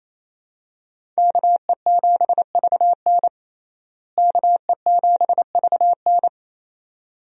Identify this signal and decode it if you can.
Morse code, heard as a keyed tone, about 27 wpm, 700 Hz. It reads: KE7VN KE7VN
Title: USA Call Signs - 5 Characters - Repeated 2x - 28wpm